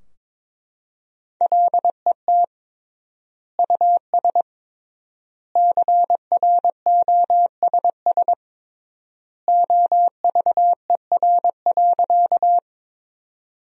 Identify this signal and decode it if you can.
Morse code, heard as a keyed tone, about 21 wpm, 700 Hz. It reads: LET US CROSS OVER.
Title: 3x Word-Spacing - Sentences from Top 300 Words - 22wpm